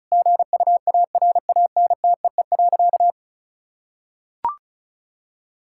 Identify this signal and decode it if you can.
Morse code, heard as a keyed tone, about 36 wpm, 700 Hz. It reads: GUARANTEE.
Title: Sentences from Top 1600 Words 35wpm